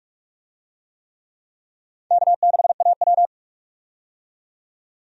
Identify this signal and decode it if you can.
Morse code, heard as a keyed tone, about 44 wpm, 700 Hz. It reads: K6AW
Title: USA Call Signs - 4 Characters - 45wpm